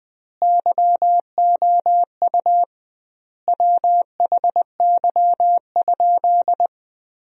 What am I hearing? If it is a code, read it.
Morse code, heard as a keyed tone, about 20 wpm, 700 Hz. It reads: YOU WHY?
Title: 2x Word-Spacing - Sentences from Top 200 Words - 20wpm